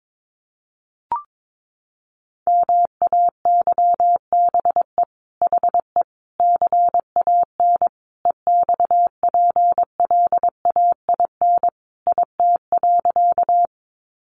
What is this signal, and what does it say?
Morse code, heard as a keyed tone, about 22 wpm, 700 Hz. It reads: MAYBE HE CAN EXPLAIN IT.
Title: Sentences from Top 1100 Words 22wpm